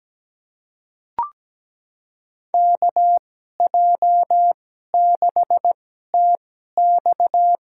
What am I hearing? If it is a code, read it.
Morse code, heard as a keyed tone, about 17 wpm, spaced as for 12 wpm, 700 Hz. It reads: KJ6TX
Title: USA Call Signs - 2x Character Spacing - 17wpm